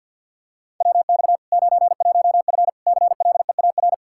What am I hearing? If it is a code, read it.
Morse code, heard as a keyed tone, about 48 wpm, 700 Hz. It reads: WX 91F CLEAR
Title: Practice QSO - Exchange 2 - Info - 50wpm